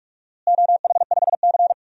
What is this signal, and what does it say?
Morse code, heard as a keyed tone, about 46 wpm, 700 Hz. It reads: OH5C
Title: DXCC-100 Call Signs - 4 Characters - Repeated 2x - 45wpm